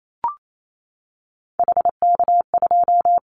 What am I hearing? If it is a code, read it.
Morse code, heard as a keyed tone, about 28 wpm, 700 Hz. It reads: HK2
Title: Callsign Prefixes for Top 100 DXCC - 28wpm